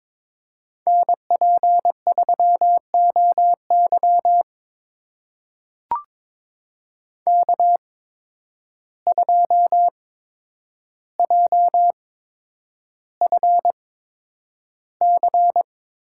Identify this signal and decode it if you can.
Morse code, heard as a keyed tone, about 22 wpm, spaced as for 9 wpm, 700 Hz. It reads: NP3OY K2JFC
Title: USA Call Signs - 5 Characters - 8x Character Spacing - 22wpm